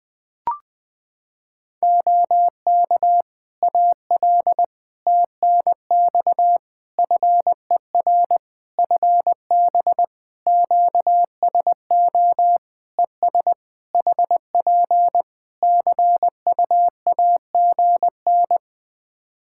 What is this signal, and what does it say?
Morse code, heard as a keyed tone, about 20 wpm, 700 Hz. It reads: OK AL TNX FER FB QSO ES HP CUAGN
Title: Practice QSO - Exchange 4 - Ending - 20wpm